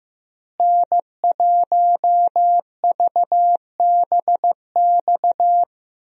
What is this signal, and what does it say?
Morse code, heard as a keyed tone, about 15 wpm, 700 Hz. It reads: N1VBX
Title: USA Call Signs - 5 Characters - Repeated 2x - 15wpm